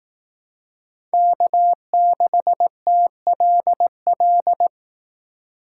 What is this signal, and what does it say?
Morse code, heard as a keyed tone, about 18 wpm, 700 Hz. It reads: K6TLL
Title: Speed Racing - USA Call Signs 15wpm